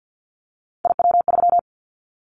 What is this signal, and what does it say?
Morse code, heard as a keyed tone, about 50 wpm, 700 Hz. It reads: IW3
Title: Callsign Prefixes for Top 50 DXCC - 50wpm